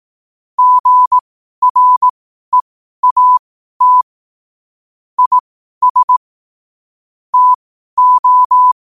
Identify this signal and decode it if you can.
Morse code, heard as a keyed tone, about 18 wpm, spaced as for 12 wpm, 1000 Hz. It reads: GREAT IS TO
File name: qotdfiles / 12wpm